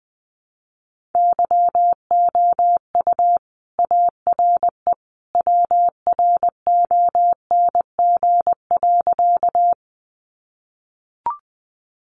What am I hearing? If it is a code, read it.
Morse code, heard as a keyed tone, about 20 wpm, 700 Hz. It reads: YOU ARE WRONG.
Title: Sentences from Top 900 Words 20wpm